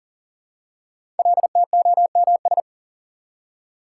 Morse code, heard as a keyed tone, about 38 wpm, 690 Hz. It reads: LTOMS
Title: Code Groups - 5 Letter:Numbers - 40wpm